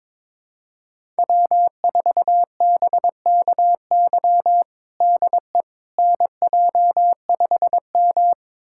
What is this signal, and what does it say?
Morse code, heard as a keyed tone, about 22 wpm, 700 Hz. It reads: W4BKY DE NJ5M
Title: USA Callsign DE Callsign 22wpm